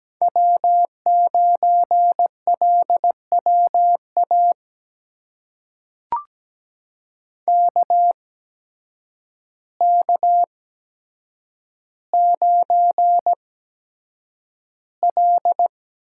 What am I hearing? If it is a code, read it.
Morse code, heard as a keyed tone, about 17 wpm, spaced as for 8 wpm, 700 Hz. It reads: W9LWA KK9L
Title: USA Call Signs - 8x Character Spacing - 17wpm